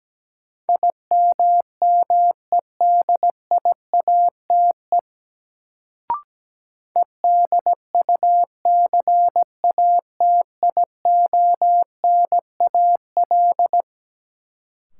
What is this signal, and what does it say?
Morse code, heard as a keyed tone, about 17 wpm, 700 Hz. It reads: IMMEDIATE EDUCATIONAL
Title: Sentences from Top 1200 Words - New Word Drill Alt - 17wpm